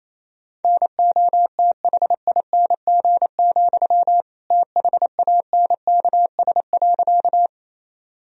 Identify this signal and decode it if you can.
Morse code, heard as a keyed tone, about 28 wpm, 700 Hz. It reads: NOTHING, THANKS.
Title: Sentences from Top 1000 Words - Review Part 3 - 28wpm